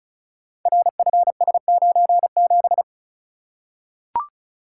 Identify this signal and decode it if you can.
Morse code, heard as a keyed tone, about 36 wpm, 700 Hz. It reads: RFS97
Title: Code Groups - 5 Letter:Numbers - 35wpm